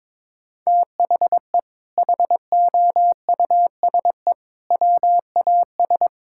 Morse code, heard as a keyed tone, about 22 wpm, 700 Hz. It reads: THE HOUSE WAS
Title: Sentences from Top 1000 Words - Review Part 6 - 22wpm